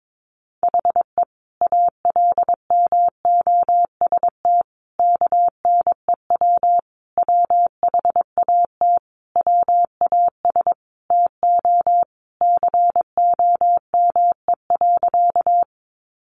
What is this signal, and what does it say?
Morse code, heard as a keyed tone, about 22 wpm, 700 Hz. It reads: HE ALMOST KNEW WHAT WAS TO COME.
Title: Sentences from Top 700 Words 22wpm